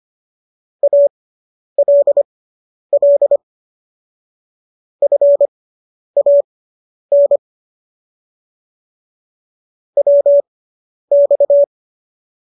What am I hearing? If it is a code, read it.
Morse code, heard as a keyed tone, about 25 wpm, spaced as for 10 wpm, 570 Hz. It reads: ALL FAN WX